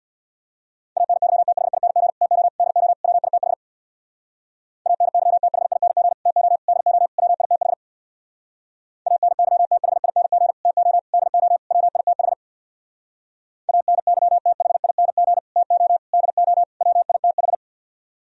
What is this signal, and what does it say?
Morse code, heard as a keyed tone, about 63 wpm, 700 Hz. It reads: ANYTHING TO DO WITH ANYTHING TO DO WITH ANYTHING TO DO WITH ANYTHING TO DO WITH